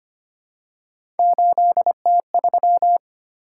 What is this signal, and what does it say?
Morse code, heard as a keyed tone, about 25 wpm, 700 Hz. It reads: 8T3
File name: Three Letter-Number Combinations - 25wpm